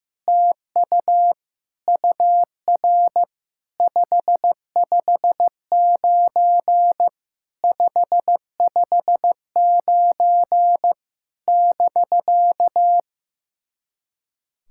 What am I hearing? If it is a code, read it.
Morse code, heard as a keyed tone, about 16 wpm, 700 Hz. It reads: TU UR 559 559 <BK>
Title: SOTA QSO Practice 15wpm